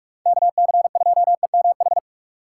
Morse code, heard as a keyed tone, about 46 wpm, 700 Hz. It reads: KC2EMH
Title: USA Call Signs - 6 Characters - 4x Character Spacing - 45wpm